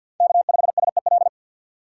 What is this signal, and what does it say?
Morse code, heard as a keyed tone, about 46 wpm, 700 Hz. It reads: K5SEL
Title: USA Call Signs - 2x Character Spacing - 50wpm